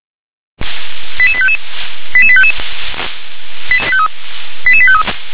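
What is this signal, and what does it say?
Five-tone selective call (ZVEI), heard as quick runs of five tones, tones of about 70 ms. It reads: 88855 88855 88853
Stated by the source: ZVEI five-tone selective call received on SDR.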